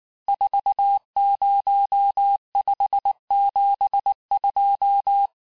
Morse code, heard as a keyed tone, about 19 wpm, 800 Hz. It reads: 40572
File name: cours-du-lundi files / cours-1595-teste-CHIFFRE-vitesse18-MPM